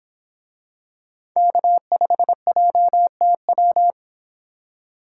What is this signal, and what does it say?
Morse code, heard as a keyed tone, about 25 wpm, 700 Hz. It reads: K5JTW